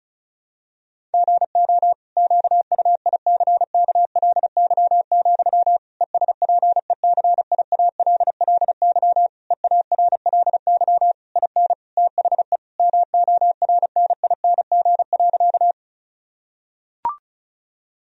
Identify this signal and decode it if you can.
Morse code, heard as a keyed tone, about 34 wpm, 700 Hz. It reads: GO QUICKLY, ESPECIALLY EARLY IN THE MORNING.
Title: Sentences from Top 1000 Words - Review Part 4 - 35wpm